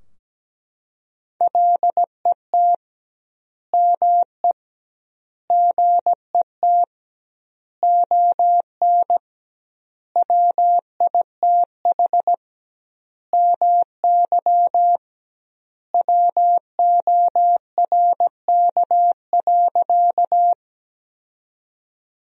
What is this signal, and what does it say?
Morse code, heard as a keyed tone, about 17 wpm, 700 Hz. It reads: LET ME GET ON WITH MY WORK.
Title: 2x Word-Spacing - Sentences from Top 300 Words - 17wpm